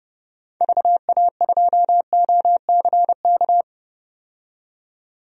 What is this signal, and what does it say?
Morse code, heard as a keyed tone, about 31 wpm, 700 Hz. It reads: VA2OCK
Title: DXCC-100 Call Signs - 6 Characters - 30wpm